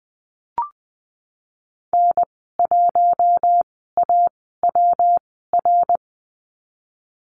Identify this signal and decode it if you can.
Morse code, heard as a keyed tone, about 19 wpm, spaced as for 14 wpm, 700 Hz. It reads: N1AWR
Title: 154 Lesson 25g - Letter p - Call Signs 20wpm